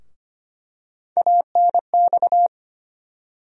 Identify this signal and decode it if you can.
Morse code, heard as a keyed tone, about 25 wpm, 700 Hz. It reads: ANX
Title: USA Call Sign Postfixes - 3 Letters 25wpm